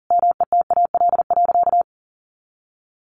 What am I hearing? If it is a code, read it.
Morse code, heard as a keyed tone, about 40 wpm, 700 Hz. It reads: METAL.
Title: A Sound Of Thunder 40wpm